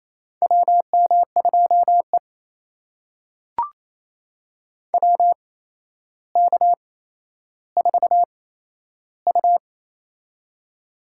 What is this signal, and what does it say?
Morse code, heard as a keyed tone, about 28 wpm, spaced as for 12 wpm, 700 Hz. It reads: WM2E WK4U
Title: USA Call Signs - 4 Characters - 8x Character Spacing - 28wpm